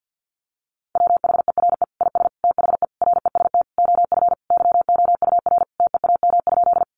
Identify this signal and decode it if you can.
Morse code, heard as a keyed tone, about 50 wpm, 700 Hz. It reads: WHERE IS THE REST OF YOUR TEAM?
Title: Sentences from Top 1500 Words 50wpm